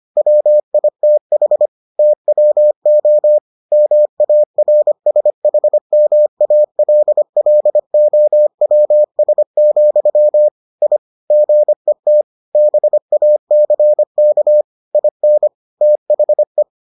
Morse code, heard as a keyed tone, about 25 wpm, 600 Hz. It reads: WITH TWO MARSHMALLOWS, I GET BACK IN THE